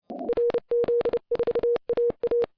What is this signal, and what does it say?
Morse code, heard as a keyed tone, about 30 wpm, 480 Hz. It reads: RZ4AA